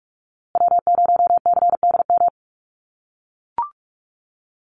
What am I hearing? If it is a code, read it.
Morse code, heard as a keyed tone, about 44 wpm, 700 Hz. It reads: W0CDM
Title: USA Call Signs Repeated 3x 45wpm